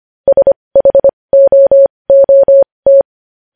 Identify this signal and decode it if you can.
Morse code, heard as a keyed tone, about 25 wpm, spaced as for 20 wpm, 550 Hz. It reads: SHOOT
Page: news-headlines-in-morse-code-at-20-wpm